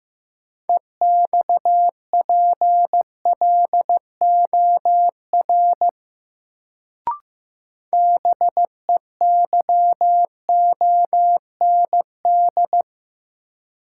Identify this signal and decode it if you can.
Morse code, heard as a keyed tone, about 15 wpm, 700 Hz. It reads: EXPLOR BEYOND
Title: Sentences from Top 1100 Words - New Word Drill Alt - 15wpm